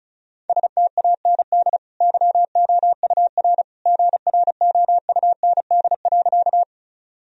Morse code, heard as a keyed tone, about 36 wpm, 700 Hz. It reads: STAND YOUR GROUND.
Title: Sentences from Top 1000 Words - Review Part 2 - 35wpm